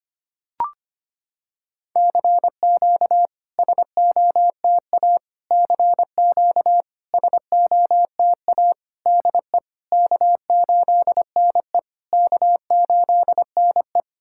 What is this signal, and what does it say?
Morse code, heard as a keyed tone, about 25 wpm, 700 Hz. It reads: CQ SOTA CQ SOTA DE K8NE K8NE
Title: SOTA QSO Practice 25wpm